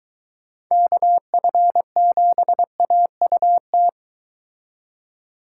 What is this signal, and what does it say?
Morse code, heard as a keyed tone, about 24 wpm, 700 Hz. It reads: KF7AUT